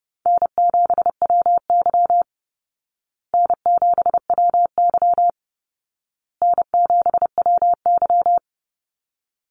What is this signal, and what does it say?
Morse code, heard as a keyed tone, about 30 wpm, 700 Hz. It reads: N7WY N7WY N7WY
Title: Easy Call Signs Repeated 3x - 30wpm